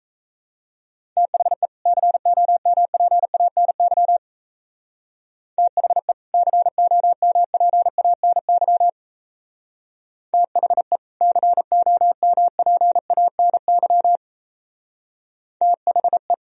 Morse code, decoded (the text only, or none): THE COMPANY THE COMPANY THE COMPANY THE